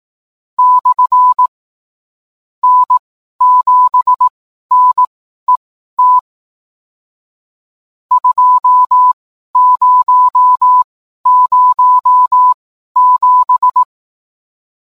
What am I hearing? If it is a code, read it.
Morse code, heard as a keyed tone, about 18 wpm, spaced as for 11 wpm, 1000 Hz. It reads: /N7NET 2007